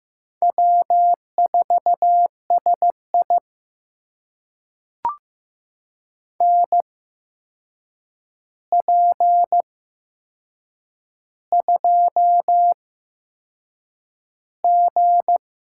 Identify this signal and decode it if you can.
Morse code, heard as a keyed tone, about 15 wpm, spaced as for 6 wpm, 700 Hz. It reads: W4SI NP2G
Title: USA Call Signs - 4 Characters - 8x Character Spacing - 15wpm